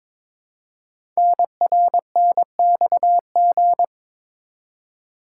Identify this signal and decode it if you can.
Morse code, heard as a keyed tone, about 22 wpm, 700 Hz. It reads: NRNXG
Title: Code Groups - 5 Letters - 22wpm